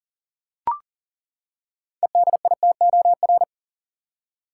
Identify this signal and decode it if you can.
Morse code, heard as a keyed tone, about 40 wpm, 700 Hz. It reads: EDITOR
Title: Six Letter Words - 40wpm